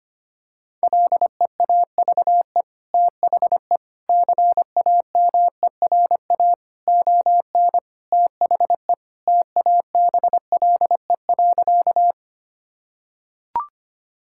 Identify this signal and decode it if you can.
Morse code, heard as a keyed tone, about 25 wpm, 700 Hz. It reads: LEAVE THE CAMERA ON THE TABLE.
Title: Sentences from Top 1000 Words - Review Part 5 - 25wpm